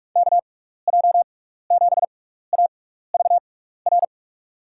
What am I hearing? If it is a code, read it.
Morse code, heard as a keyed tone, about 41 wpm, spaced as for 15 wpm, 700 Hz. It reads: KJ7AVR
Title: USA Call Signs - 6x Character Spacing - 45wpm